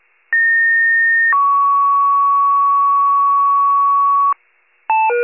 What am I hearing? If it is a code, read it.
Two-tone page, heard as a one-second tone, then a three-second tone, about 1850 then 1120 Hz. It is heard from 0.3 to 4.3 s.